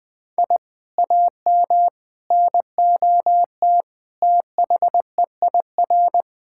Morse code, heard as a keyed tone, about 20 wpm, 700 Hz. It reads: I AM NOT THEIR
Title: Sentences from Top 1000 Words - Review Part 6 - 20wpm